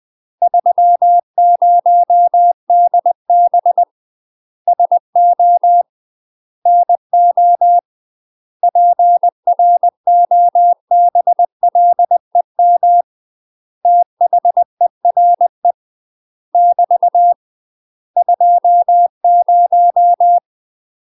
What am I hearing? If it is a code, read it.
Morse code, heard as a keyed tone, about 20 wpm, 700 Hz. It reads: 30DB SO NO PROBLEM THERE <BT> 20
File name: UK MASTER QSOs TRAINING / 20WPM / UK MASTER QSOs 0026